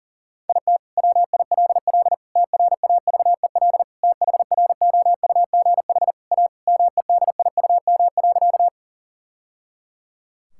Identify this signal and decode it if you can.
Morse code, heard as a keyed tone, about 40 wpm, 700 Hz. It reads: IT WILL TRAVEL THROUGH A MEDIUM.
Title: Sentences from Top 1000 Words - Review Part 6 - 40wpm